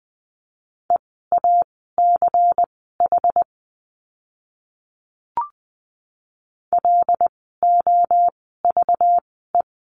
Morse code, heard as a keyed tone, about 20 wpm, spaced as for 14 wpm, 700 Hz. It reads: EACH LOVE
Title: 192 Lesson 31f - Letter v - Words 20wpm